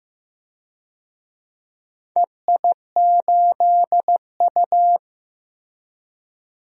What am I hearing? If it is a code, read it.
Morse code, heard as a keyed tone, about 15 wpm, 700 Hz. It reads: EI8U
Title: DXCC-100 Call Signs - 4 Characters - Repeated 2x - 15wpm